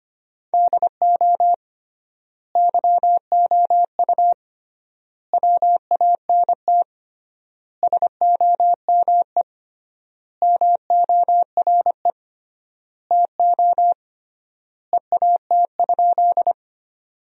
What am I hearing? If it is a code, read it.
Morse code, heard as a keyed tone, about 25 wpm, 700 Hz. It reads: DO YOU WANT SOME MORE TO EAT?
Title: 3x Word-Spacing - Sentences from Top 300 Words - 25wpm